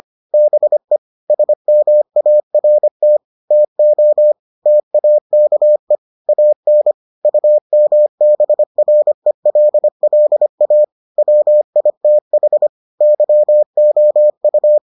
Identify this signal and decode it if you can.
Morse code, heard as a keyed tone, about 25 wpm, 600 Hz. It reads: BE SMART TO TAKE AN UMBRELLA WITH YOU